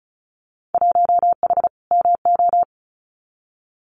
Morse code, heard as a keyed tone, about 36 wpm, 690 Hz. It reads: 1H MO
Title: Winter Field Day Exchanges 35wpm